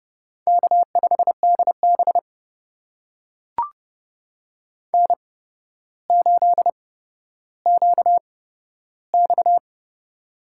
Morse code, heard as a keyed tone, about 30 wpm, spaced as for 13 wpm, 700 Hz. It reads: K5DB N8QX
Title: USA Call Signs - 4 Characters - 8x Character Spacing - 30wpm